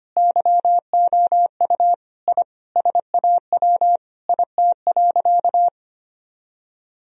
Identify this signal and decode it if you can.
Morse code, heard as a keyed tone, about 25 wpm, 700 Hz. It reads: YOU I SAW IT.
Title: Sentences from Top 1000 Words - Review Part 4 - 25wpm